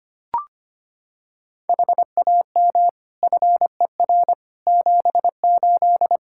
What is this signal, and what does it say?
Morse code, heard as a keyed tone, about 25 wpm, 700 Hz. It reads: HAM FER 78
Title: Practice QSO - Exchange 3 - More Info - 25wpm